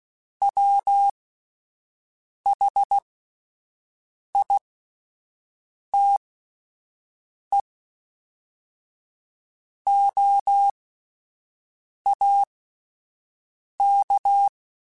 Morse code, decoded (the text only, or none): WHITE OAK